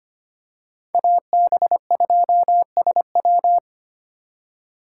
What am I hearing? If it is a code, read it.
Morse code, heard as a keyed tone, about 25 wpm, 700 Hz. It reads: AB2SW
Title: USA Call Signs - 6x Character Spacing - 25wpm